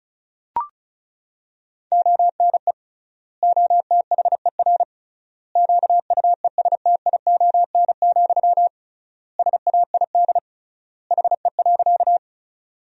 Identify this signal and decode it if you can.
Morse code, heard as a keyed tone, about 36 wpm, 700 Hz. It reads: ONE OTHER QUESTION, SAID HE.
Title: 3x Word-Spacing - Sentences from Top 400 Words - 35wpm